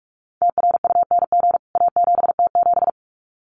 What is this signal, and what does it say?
Morse code, heard as a keyed tone, about 44 wpm, 700 Hz. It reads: TRUNG A7T7
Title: CWT Exchange 45wpm